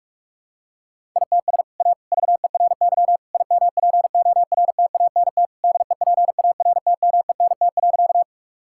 Morse code, heard as a keyed tone, about 42 wpm, 700 Hz. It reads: ITS A VERY IMPORTANT DEPARTMENT.
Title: Sentences from Top 1000 Words - Review Part 5 - 45wpm